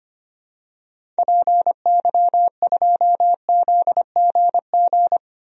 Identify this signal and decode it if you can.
Morse code, heard as a keyed tone, about 25 wpm, 700 Hz. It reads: PY2ZGG
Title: DXCC-100 Call Signs - 6 Characters - Repeated 2x - 25wpm